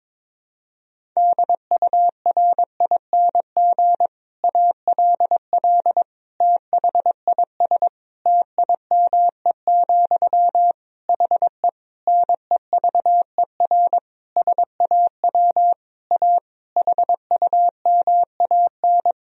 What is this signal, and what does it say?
Morse code, heard as a keyed tone, about 22 wpm, 700 Hz. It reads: DURING ALL THIS TIME, HE NEVER SAW A HUMAN
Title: Sentences from Top 1000 Words - Review Part 3 - 22wpm